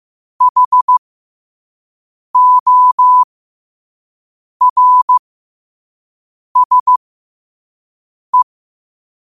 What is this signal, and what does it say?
Morse code, heard as a keyed tone, about 15 wpm, spaced as for 5 wpm, 1000 Hz. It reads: HORSE
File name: qotdfiles / 5wpm